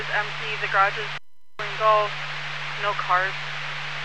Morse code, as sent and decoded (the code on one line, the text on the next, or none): none